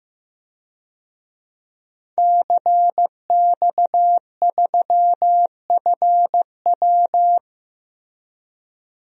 Morse code, heard as a keyed tone, about 15 wpm, 700 Hz. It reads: CX3FW
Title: DXCC-100 Call Signs - 5 Characters - 15wpm